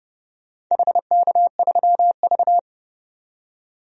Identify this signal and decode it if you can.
Morse code, heard as a keyed tone, about 30 wpm, 700 Hz. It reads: HK3V
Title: DXCC-100 Call Signs - 4 Characters - 30wpm